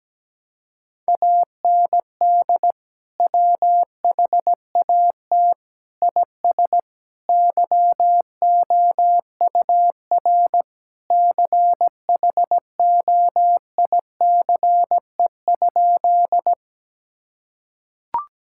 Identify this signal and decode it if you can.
Morse code, heard as a keyed tone, about 17 wpm, 700 Hz. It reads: AND WHAT IS YOUR CHOICE?
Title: Sentences from Top 1000 Words - Review Part 6 - 17wpm